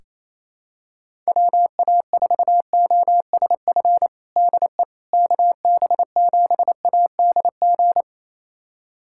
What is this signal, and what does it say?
Morse code, heard as a keyed tone, about 28 wpm, 700 Hz. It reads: WA4OSF DE KB7ADG
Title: USA Callsign DE Callsign - 6 Characters - 28wpm